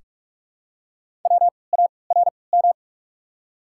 Morse code, heard as a keyed tone, about 44 wpm, spaced as for 25 wpm, 690 Hz. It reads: WARM
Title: 3x Character Spacing - Top 500 Words - 45wpm